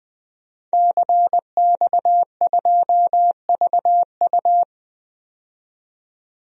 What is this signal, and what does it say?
Morse code, heard as a keyed tone, about 20 wpm, 700 Hz. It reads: CX2VU
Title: DXCC-100 Call Signs - 5 Characters - 20wpm